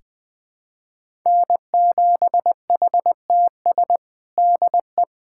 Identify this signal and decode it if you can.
Morse code, heard as a keyed tone, about 20 wpm, 700 Hz. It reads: N7HTS DE
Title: USA Callsign DE Callsign 20wpm